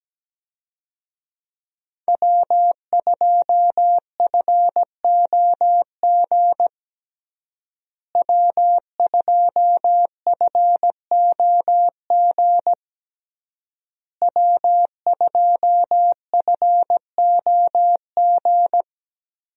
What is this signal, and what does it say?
Morse code, heard as a keyed tone, about 17 wpm, 700 Hz. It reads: W2FOG W2FOG W2FOG
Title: USA Call Signs - 5 Characters - Repeated 3x - 17wpm